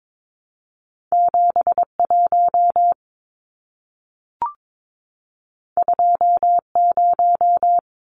Morse code, heard as a keyed tone, about 22 wpm, 700 Hz. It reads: 71 20
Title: Numbers - 2-Digit 22wpm